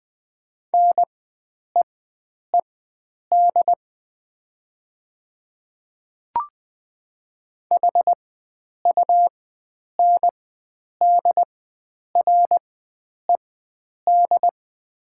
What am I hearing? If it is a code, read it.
Morse code, heard as a keyed tone, about 20 wpm, spaced as for 9 wpm, 700 Hz. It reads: NEED HUNDRED
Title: 4x Character Spacing - Top 500 Words - 20wpm